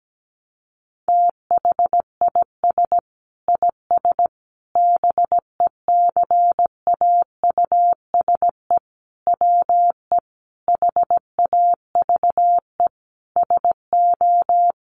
Morse code, heard as a keyed tone, about 17 wpm, 700 Hz. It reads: THIS IS BECAUSE WE HAVE SO
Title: Sentences from Top 1200 Words 17wpm